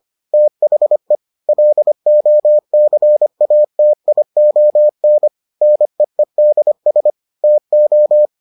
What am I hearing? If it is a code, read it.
Morse code, heard as a keyed tone, about 25 wpm, 600 Hz. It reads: THE LOCATION NEEDS TO